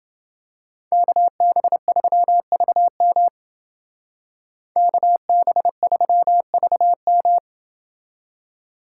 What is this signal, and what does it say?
Morse code, heard as a keyed tone, about 29 wpm, 700 Hz. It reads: KB3VM KB3VM